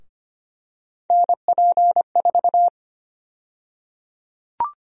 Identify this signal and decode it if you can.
Morse code, heard as a keyed tone, about 25 wpm, 700 Hz. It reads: NP4
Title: USA Call Sign Prefixes - Repeated 2x - 25wpm